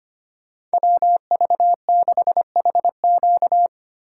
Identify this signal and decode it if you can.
Morse code, heard as a keyed tone, about 25 wpm, 700 Hz. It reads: WV6HQ
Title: USA Call Signs - 5 Characters - 3x Character Spacing - 25wpm